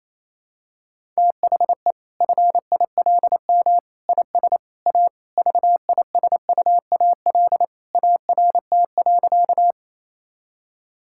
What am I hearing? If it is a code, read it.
Morse code, heard as a keyed tone, about 28 wpm, 700 Hz. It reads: THE FILM IS A VISUAL ART.
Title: Sentences from Top 1000 Words - Review Part 6 - 28wpm